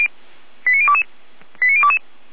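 Five-tone selective call (ZVEI), heard as quick runs of five tones, tones of about 70 ms. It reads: 89022 89022
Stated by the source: ZVEI five-tone call 89011 received by SDR.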